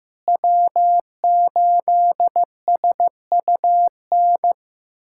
Morse code, heard as a keyed tone, about 15 wpm, 700 Hz. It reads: W8SUN
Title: USA Callsign DE Callsign - 5 Characters - 15wpm